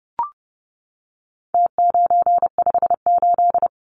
Morse code, heard as a keyed tone, about 30 wpm, 700 Hz. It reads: T958
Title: Numbers - Contest Style - 4-Digit 30wpm